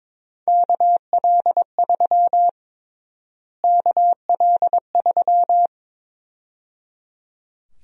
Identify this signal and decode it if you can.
Morse code, heard as a keyed tone, about 22 wpm, 700 Hz. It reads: KL3 KL3
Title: USA Call Sign Prefixes - Repeated 2x - 22wpm